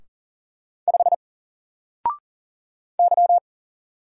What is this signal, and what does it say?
Morse code, heard as a keyed tone, about 40 wpm, 700 Hz. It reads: 5 Y
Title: Single Character (Letter:Number:Punctuation:Prosign) 40wpm